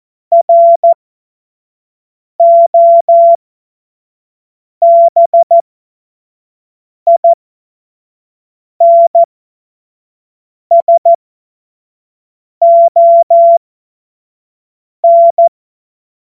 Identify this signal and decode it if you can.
Morse code, heard as a keyed tone, about 14 wpm, spaced as for 5 wpm, 680 Hz. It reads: ROBINSON
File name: Morsekurs / Witze / Witz 20 Mai 2025 25 F70